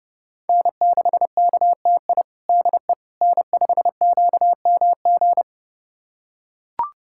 Morse code, heard as a keyed tone, about 30 wpm, 700 Hz. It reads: N6KTI DE N5QMG
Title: USA Callsign DE Callsign - 5 Characters - 30wpm